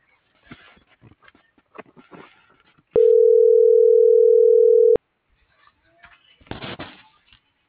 Ringback tone (North America), 440 + 480 Hz.